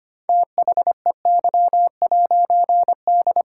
Morse code, heard as a keyed tone, about 25 wpm, 700 Hz. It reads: THEY'D
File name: practice / 25wpm / All-Age Appropriate Jokes - 25wpm